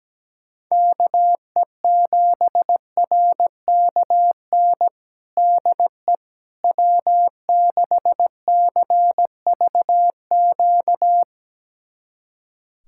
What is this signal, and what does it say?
Morse code, heard as a keyed tone, about 17 wpm, 700 Hz. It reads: KE7RKN DE W6CVQ
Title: USA Callsign DE Callsign 17wpm